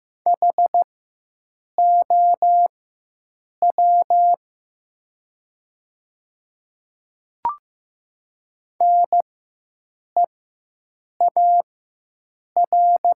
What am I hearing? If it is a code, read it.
Morse code, heard as a keyed tone, about 15 wpm, spaced as for 7 wpm, 690 Hz. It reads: HOW NEAR
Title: 4x Character Spacing - Top 200 Words - 15wpm